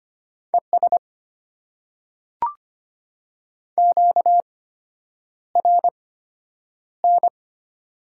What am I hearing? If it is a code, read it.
Morse code, heard as a keyed tone, about 25 wpm, spaced as for 9 wpm, 690 Hz. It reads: ES QRN
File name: Top 90 QSO Elements - 8x Character Spacing - 25wpm